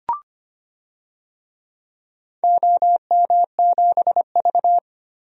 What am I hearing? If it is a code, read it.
Morse code, heard as a keyed tone, about 25 wpm, 700 Hz. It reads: OM7V
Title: DXCC-100 Call Signs - 4 Characters - 25wpm